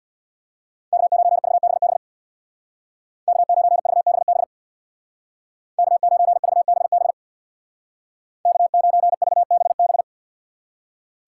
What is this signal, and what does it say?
Morse code, heard as a keyed tone, about 62 wpm, 700 Hz. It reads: K9VBB K9VBB K9VBB K9VBB
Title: Speed Racing - USA 1x3 Call Signs 50wpm